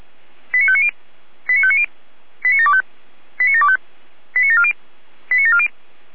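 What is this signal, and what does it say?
Five-tone selective call (ZVEI), heard as quick runs of five tones, tones of about 70 ms. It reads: 89590 89590 89725 89725 89740 89740
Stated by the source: ZVEI five-tone selective call received on an SDR.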